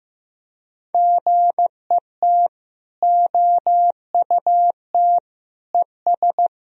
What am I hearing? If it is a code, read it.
Morse code, heard as a keyed tone, about 15 wpm, 700 Hz. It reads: GET OUT ES
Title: Sentences from Top 200 Words and Abbreviations 15wpm